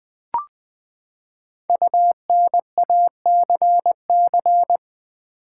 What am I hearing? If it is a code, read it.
Morse code, heard as a keyed tone, about 20 wpm, 700 Hz. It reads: UNACC
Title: Build A Word - 5 Letters At A Time - 12-Letter Words - 20wpm